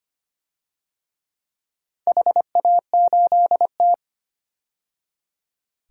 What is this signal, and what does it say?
Morse code, heard as a keyed tone, about 25 wpm, 700 Hz. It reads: HA8T
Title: DXCC-100 Call Signs - 4 Characters - 25wpm